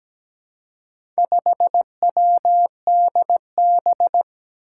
Morse code, heard as a keyed tone, about 17 wpm, 700 Hz. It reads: 5WDB
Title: Four Challenging Letter-Number Combinations 17wpm